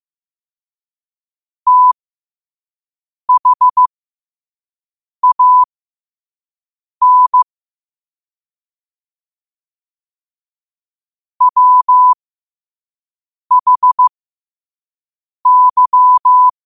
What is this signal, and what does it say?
Morse code, heard as a keyed tone, about 15 wpm, spaced as for 5 wpm, 1000 Hz. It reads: THAN WHY
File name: qotdfiles / 5wpm